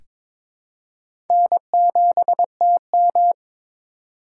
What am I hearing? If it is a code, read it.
Morse code, heard as a keyed tone, about 22 wpm, 700 Hz. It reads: N7TM
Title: USA Call Signs - 4 Characters - 3x Character Spacing - 22wpm